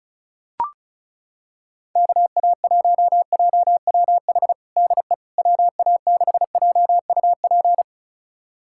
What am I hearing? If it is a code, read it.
Morse code, heard as a keyed tone, about 36 wpm, 700 Hz. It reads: KA1JWH DE WA6JUP
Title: USA Callsign DE Callsign - 6 Characters - 35wpm